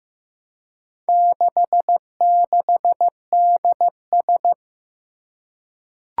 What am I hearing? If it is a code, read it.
Morse code, heard as a keyed tone, about 15 wpm, 700 Hz. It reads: 66DS
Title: Four Challenging Letter-Number Combinations 15wpm